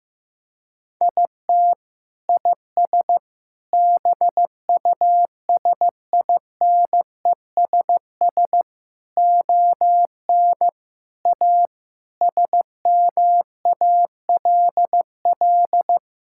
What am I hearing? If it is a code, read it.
Morse code, heard as a keyed tone, about 15 wpm, 700 Hz. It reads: IT IS BUSINESS ON A SMALL
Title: Sentences from Top 1000 Words - Review Part 5 - 15wpm